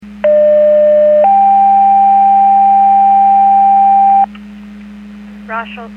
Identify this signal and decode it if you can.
Two-tone page, heard as a one-second tone, then a three-second tone, about 600 then 790 Hz, with speech. It is heard from 0.2 to 4.3 s.